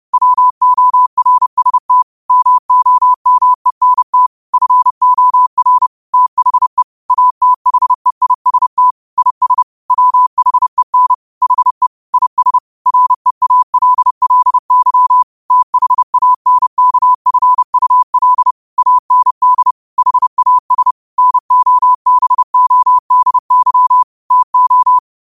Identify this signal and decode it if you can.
Morse code, heard as a keyed tone, about 30 wpm, 1000 Hz. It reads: WORST MOMENT FOR THE ATHEIST IS WHEN HE IS REALLY THANKFUL AND HAS NOBODY TO